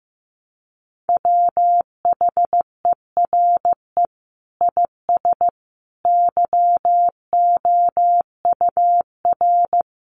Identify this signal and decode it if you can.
Morse code, heard as a keyed tone, about 15 wpm, 700 Hz. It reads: WHERE IS YOUR
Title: Sentences from Top 1500 Words 15wpm